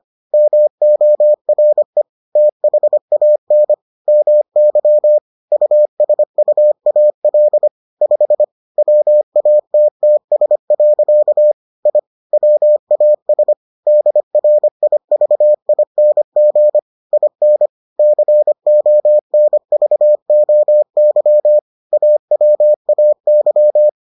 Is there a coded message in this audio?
Yes: Morse code from 0.3 to 23.9 s, heard as a keyed tone, about 25 wpm, 600 Hz. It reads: MORE THAN MY USUAL 5 WATTS. I WAS DRIVING IN CONVOY AWAY